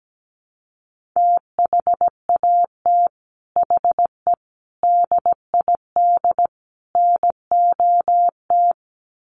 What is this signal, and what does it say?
Morse code, heard as a keyed tone, about 17 wpm, 700 Hz. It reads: THAT HE DID NOT
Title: Sets of 4 Words 17wpm